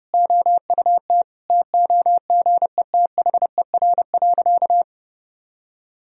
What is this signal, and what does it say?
Morse code, heard as a keyed tone, about 32 wpm, 700 Hz. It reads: OUT TOGETHER.
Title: Sentences from Top 1000 Words - Review Part 1 - 30wpm